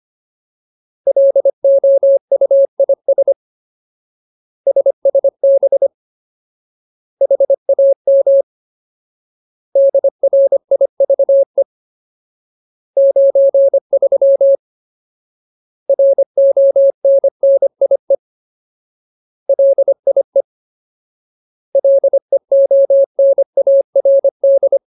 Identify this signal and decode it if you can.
Morse code, heard as a keyed tone, about 25 wpm, 550 Hz. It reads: LOUIS SSB HAM DRIVE 93 RONNIE LIE LEONARD
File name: Ultimate Word List / 25WPM Ultimate